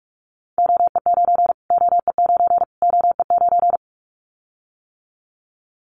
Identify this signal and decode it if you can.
Morse code, heard as a keyed tone, about 44 wpm, 700 Hz. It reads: OE9 OE9 OE9
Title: Callsign Prefixes for Top 25 DXCC - Repeated 3x - 45wpm